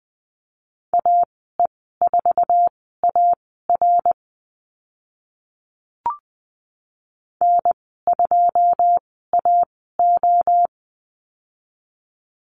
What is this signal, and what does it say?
Morse code, heard as a keyed tone, about 20 wpm, spaced as for 14 wpm, 700 Hz. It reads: AE4AR N2AO
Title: 082 Lesson 14g - Number 2 - Call Signs 20wpm